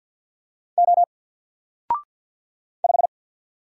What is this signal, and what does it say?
Morse code, heard as a keyed tone, about 49 wpm, 700 Hz. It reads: O 5
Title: Single Letter-Number Alt 50wpm